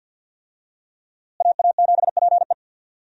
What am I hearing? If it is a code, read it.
Morse code, heard as a keyed tone, about 53 wpm, 700 Hz. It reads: AA7PE
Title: USA Call Signs - 5 Characters - Repeated 3x - 50wpm